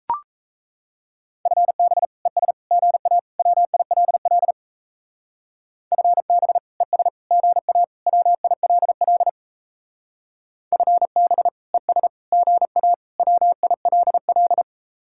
Morse code, heard as a keyed tone, about 34 wpm, 700 Hz. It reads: FB ES GA WILL FB ES GA WILL FB ES GA WILL
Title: Speed-Racing Common QSO Phrases - 28wpm